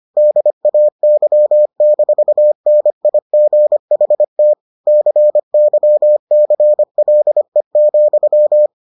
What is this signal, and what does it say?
Morse code, heard as a keyed tone, about 25 wpm, 600 Hz. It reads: DAY-NIGHT CYCLE,